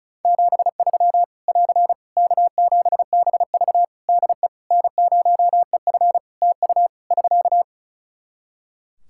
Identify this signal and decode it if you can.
Morse code, heard as a keyed tone, about 34 wpm, 700 Hz. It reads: 73 <AR> K7BV DE N0EF TU <SK>
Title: Practice QSO - Exchange 4 - Ending - 35wpm